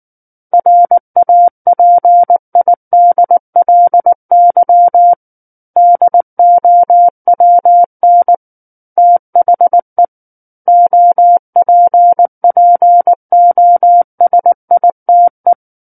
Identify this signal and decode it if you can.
Morse code, heard as a keyed tone, about 19 wpm, 700 Hz. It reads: RAPIDLY DOWN THE OPPOSITE